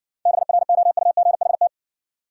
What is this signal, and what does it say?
Morse code, heard as a keyed tone, about 60 wpm, 700 Hz. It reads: BROUGHT